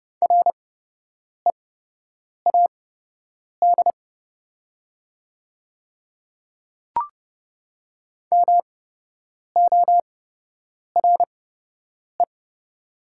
Morse code, heard as a keyed tone, about 30 wpm, spaced as for 8 wpm, 700 Hz. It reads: READ MORE
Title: 8x Character Spacing - Top 300 Words - 30wpm